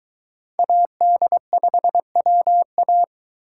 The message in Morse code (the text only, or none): AD5WA